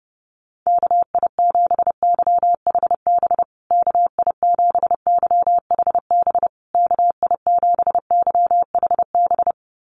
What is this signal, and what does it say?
Morse code, heard as a keyed tone, about 30 wpm, 700 Hz. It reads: KI7YHB KI7YHB KI7YHB
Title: USA Call Signs Repeated 3x 30wpm